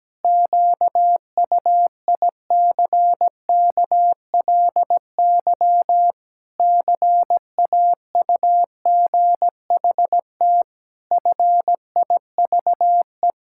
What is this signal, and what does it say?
Morse code, heard as a keyed tone, about 17 wpm, 700 Hz. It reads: QUICKLY CAUGHT FIVE
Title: Pangram Challenge v2 - 17wpm